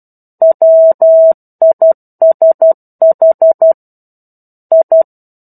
Morse code, heard as a keyed tone, about 12 wpm, 650 Hz. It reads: WISH I
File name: 01-Chapter-01 12 WPM